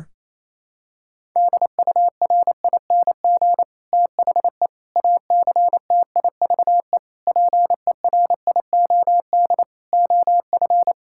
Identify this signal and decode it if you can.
Morse code, heard as a keyed tone, about 28 wpm, 700 Hz. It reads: DURING THE ACTIVE PERIOD OF
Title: Sentences from Top 1000 Words - Review Part 6 - 28wpm